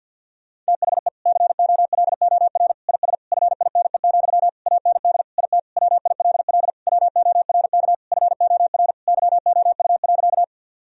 Morse code, heard as a keyed tone, about 50 wpm, 700 Hz. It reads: THE COLOR IS FINE, AND IT WILL WORK FOR YOU.
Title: Sentences from Top 1000 Words - Review Part 2 - 50wpm